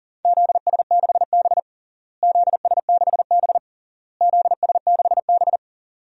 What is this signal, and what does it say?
Morse code, heard as a keyed tone, about 40 wpm, 700 Hz. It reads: ZS6B ZS6B ZS6B
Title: DXCC-100 Call Signs - 4 Characters - Repeated 3x - 40wpm